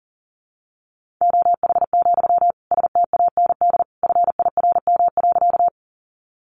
Morse code, heard as a keyed tone, about 40 wpm, 700 Hz. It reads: OH, STAND FIRM.
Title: Sentences from Top 1400 Words 40wpm